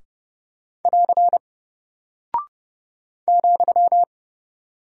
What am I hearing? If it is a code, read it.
Morse code, heard as a keyed tone, about 30 wpm, 700 Hz. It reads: <AR> ,
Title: Single Character (Letter:Number:Punctuation:Prosign) Alt 30wpm